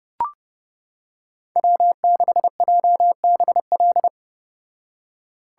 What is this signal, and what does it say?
Morse code, heard as a keyed tone, about 29 wpm, 700 Hz. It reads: W6JBL